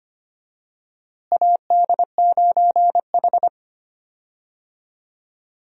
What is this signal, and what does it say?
Morse code, heard as a keyed tone, about 25 wpm, 700 Hz. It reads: AD9H
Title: USA Call Signs - 4 Characters - Repeated 3x - 25wpm